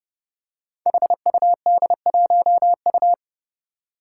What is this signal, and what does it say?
Morse code, heard as a keyed tone, about 30 wpm, 700 Hz. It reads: HUD1U
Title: Five Challenging Letter-Number Combinations 30wpm